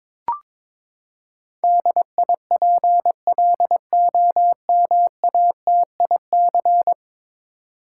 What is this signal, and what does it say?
Morse code, heard as a keyed tone, about 22 wpm, 700 Hz. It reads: DIPLOMATIC
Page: morse-code-ninja-practice-22wpm